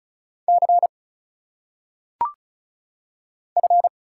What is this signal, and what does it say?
Morse code, heard as a keyed tone, about 36 wpm, 700 Hz. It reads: C F
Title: Single Letters - Phonetic Pronunciation 35wpm